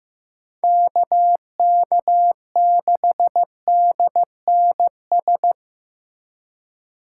Morse code, heard as a keyed tone, about 18 wpm, 700 Hz. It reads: KK6DNS
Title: Speed Racing - USA 2x3 Call Signs 15wpm